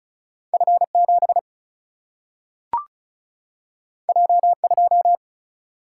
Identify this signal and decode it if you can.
Morse code, heard as a keyed tone, about 35 wpm, 700 Hz. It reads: F7 J2
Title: Single-Letter Single-Number 35wpm